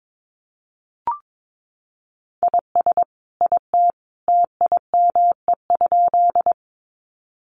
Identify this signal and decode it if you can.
Morse code, heard as a keyed tone, about 22 wpm, 700 Hz. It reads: IS IT TIME?
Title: Sentences from Top 100 Words 22wpm